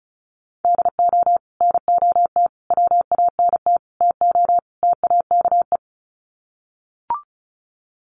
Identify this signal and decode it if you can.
Morse code, heard as a keyed tone, about 34 wpm, 700 Hz. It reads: DO NOT WANT TO TAKE
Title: Sets of 5 Words Encore2 35wpm